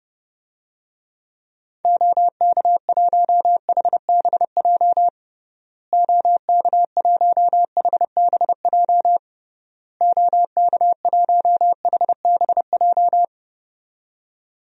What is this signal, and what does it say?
Morse code, heard as a keyed tone, about 30 wpm, 700 Hz. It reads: OK1HBJ OK1HBJ OK1HBJ
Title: DXCC-100 Call Signs - 6 Characters - Repeated 3x - 30wpm